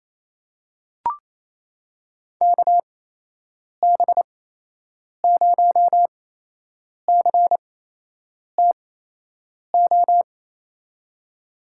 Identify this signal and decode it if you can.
Morse code, heard as a keyed tone, about 28 wpm, spaced as for 8 wpm, 700 Hz. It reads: KB0CTO
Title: USA Call Signs - 6 Characters - 8x Character Spacing - 28wpm